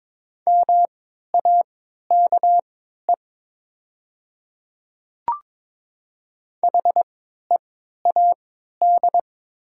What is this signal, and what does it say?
Morse code, heard as a keyed tone, about 22 wpm, spaced as for 12 wpm, 700 Hz. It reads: MAKE HEAD
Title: 3x Character Spacing - Top 500 Words - 22wpm